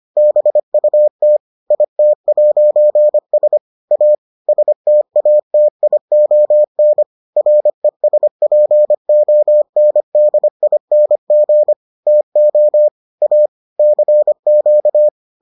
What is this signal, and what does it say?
Morse code, heard as a keyed tone, about 25 wpm, 600 Hz. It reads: BUT IT'S A STATION RESPONDING TO A CQ